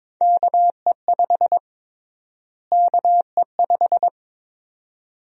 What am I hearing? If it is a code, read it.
Morse code, heard as a keyed tone, about 22 wpm, 700 Hz. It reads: KE5 KE5
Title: USA Call Sign Prefixes - Repeated 2x - 22wpm